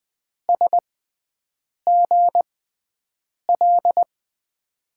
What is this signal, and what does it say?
Morse code, heard as a keyed tone, about 20 wpm, spaced as for 7 wpm, 700 Hz. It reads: SGL
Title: USA Call Signs - 6x Character Spacing - 20wpm